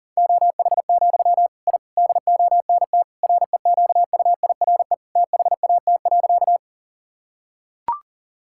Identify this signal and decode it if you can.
Morse code, heard as a keyed tone, about 40 wpm, 700 Hz. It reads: OH, I DONT REQUIRE THAT.
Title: Sentences from Top 1000 Words - Review Part 6 - 40wpm